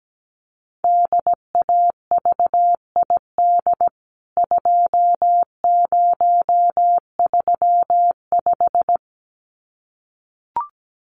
Morse code, heard as a keyed tone, about 17 wpm, 700 Hz. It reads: DAVID 2035
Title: CWT Exchange 17wpm